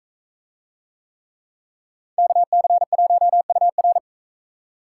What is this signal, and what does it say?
Morse code, heard as a keyed tone, about 42 wpm, 700 Hz. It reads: KC1UR